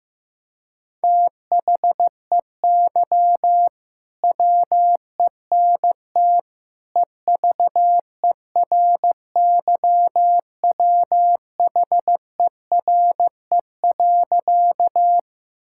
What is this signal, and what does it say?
Morse code, heard as a keyed tone, about 15 wpm, 700 Hz. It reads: THEY WENT EVERYWHERE.
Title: Sentences from Top 1600 Words 15wpm